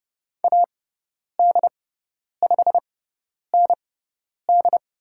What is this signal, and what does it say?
Morse code, heard as a keyed tone, about 30 wpm, spaced as for 10 wpm, 700 Hz. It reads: AD5ND
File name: Lesson 17g - Letter u - Call Signs 30wpm